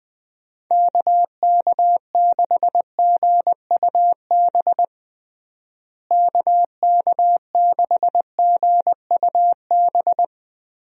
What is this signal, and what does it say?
Morse code, heard as a keyed tone, about 20 wpm, 700 Hz. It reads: KK6GUB KK6GUB
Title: USA Call Signs - 6 Characters - Repeated 3x - 20wpm